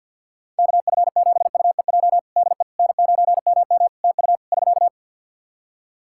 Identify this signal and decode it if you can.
Morse code, heard as a keyed tone, about 46 wpm, 700 Hz. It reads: KF7UEJ DE N9MM TU <SK>
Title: Practice QSO - Exchange 4 - Ending - 50wpm